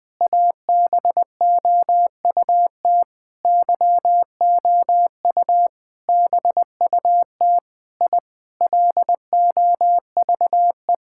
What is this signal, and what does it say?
Morse code, heard as a keyed tone, about 21 wpm, 700 Hz. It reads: ABOUT YOU BUT I LOVE
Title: Sentences with 7-Word Phrases - 20wpm